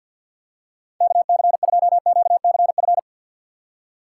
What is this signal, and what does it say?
Morse code, heard as a keyed tone, about 52 wpm, 700 Hz. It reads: KC2QCF
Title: USA Call Signs - 6 Characters - 4x Character Spacing - 50wpm